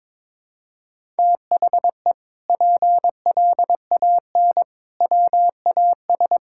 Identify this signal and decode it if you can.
Morse code, heard as a keyed tone, about 22 wpm, 700 Hz. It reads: THE PLAN WAS
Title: Sentences from Top 1000 Words - Review Part 6 - 22wpm